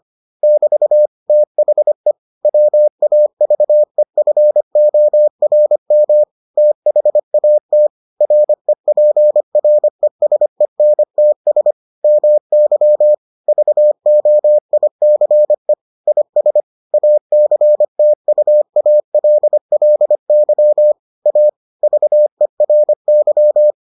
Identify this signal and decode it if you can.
Morse code, heard as a keyed tone, about 25 wpm, 600 Hz. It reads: <BT> THE WAVEFORM THAT REPRESENTS MY VOICE IS ACTUALLY A VERY